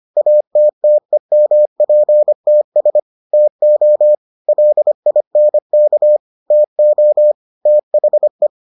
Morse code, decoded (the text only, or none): ATTEMPTS TO LINK TO THE